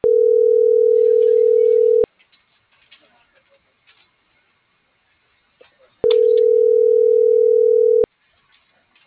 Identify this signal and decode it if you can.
Ringback tone (North America), 440 + 480 Hz.